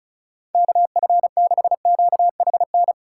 Morse code, heard as a keyed tone, about 36 wpm, 700 Hz. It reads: KF6QHN
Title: USA Call Signs - 6 Characters - 2x Character Spacing - 35wpm